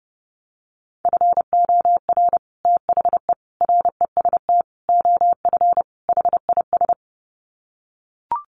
Morse code, heard as a keyed tone, about 30 wpm, 700 Hz. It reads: FOR THE REST OF HIS
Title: Sets of 5 Words 30wpm